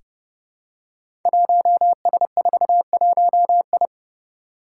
Morse code, heard as a keyed tone, about 30 wpm, 700 Hz. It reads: 1S41I
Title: Five Challenging Letter-Number Combinations 30wpm